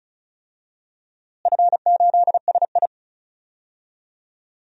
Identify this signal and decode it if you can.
Morse code, heard as a keyed tone, about 34 wpm, 700 Hz. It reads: F8SI
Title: DXCC-100 Call Signs - 4 Characters - Repeated 3x - 35wpm